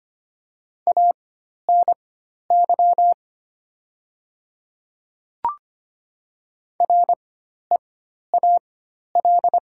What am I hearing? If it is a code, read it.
Morse code, heard as a keyed tone, about 25 wpm, spaced as for 12 wpm, 700 Hz. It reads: ANY REAL
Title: 4x Character Spacing - Top 500 Words - 25wpm